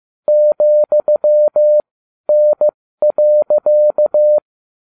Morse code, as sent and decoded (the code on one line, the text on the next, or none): --..-- -. .-.-.-
,N.